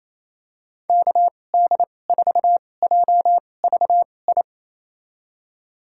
Morse code, heard as a keyed tone, about 28 wpm, spaced as for 20 wpm, 700 Hz. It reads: KD4JVI
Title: USA Call Signs - 6 Characters - 2x Character Spacing - 28wpm